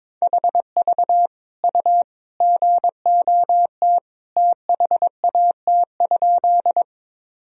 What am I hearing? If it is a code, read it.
Morse code, heard as a keyed tone, about 21 wpm, 700 Hz. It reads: HV U GOT THAT?
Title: Sentences from Top 300 Words and Abbreviations 22wpm